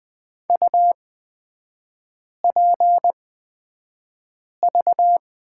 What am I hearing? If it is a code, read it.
Morse code, heard as a keyed tone, about 20 wpm, spaced as for 5 wpm, 700 Hz. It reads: UPV
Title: 243 Lesson 39e - Letter z - Three Characters 20wpm